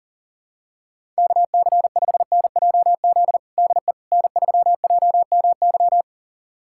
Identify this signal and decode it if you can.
Morse code, heard as a keyed tone, about 40 wpm, 700 Hz. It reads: KC5NJZ DE N3JMY
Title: USA Callsign DE Callsign 40wpm